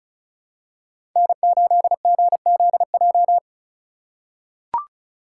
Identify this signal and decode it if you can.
Morse code, heard as a keyed tone, about 34 wpm, 700 Hz. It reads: N8GZJ
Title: USA Call Signs - 5 Characters - 3x Character Spacing - 35wpm